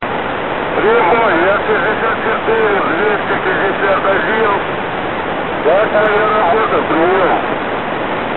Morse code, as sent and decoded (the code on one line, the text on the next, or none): none